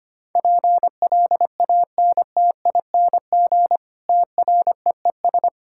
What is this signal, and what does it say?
Morse code, heard as a keyed tone, about 25 wpm, 700 Hz. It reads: PLANTING TREES
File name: practice / 25wpm / Difficult Sentences with Hints - 25wpm